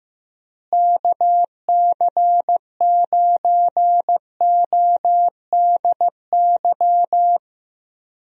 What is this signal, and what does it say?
Morse code, heard as a keyed tone, about 15 wpm, 700 Hz. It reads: KC9ODY
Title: USA Call Signs - 6 Characters - Repeated 3x - 15wpm